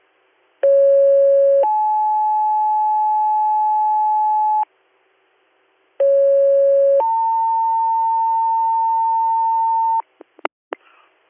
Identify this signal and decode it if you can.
Two-tone page, heard as a one-second tone, then a three-second tone, about 550 then 860 Hz, about 550 then 900 Hz, with noise. It is heard from 0.6 to 10.0 s.